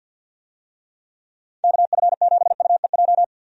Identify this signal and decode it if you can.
Morse code, heard as a keyed tone, about 51 wpm, 700 Hz. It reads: KF7UEJ
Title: USA Call Signs - 6 Characters - 50wpm